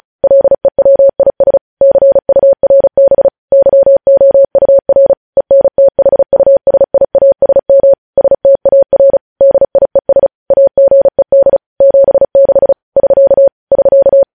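Morse code, heard as a keyed tone, about 36 wpm, 550 Hz. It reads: LEWIS CURB YOUR ENTHUSIASM STAR DIES AGED 76 <SK> <SK>
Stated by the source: Morse code news headlines sent at 35 wpm.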